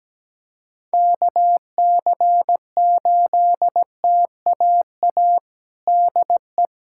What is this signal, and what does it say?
Morse code, heard as a keyed tone, about 17 wpm, 700 Hz. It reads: KC8TAA DE
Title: USA Callsign DE Callsign - 6 Characters - 17wpm